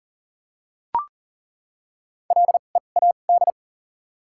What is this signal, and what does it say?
Morse code, heard as a keyed tone, about 40 wpm, spaced as for 29 wpm, 700 Hz. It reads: LEAD
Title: 2x Character Spacing - Top 500 Words - 40wpm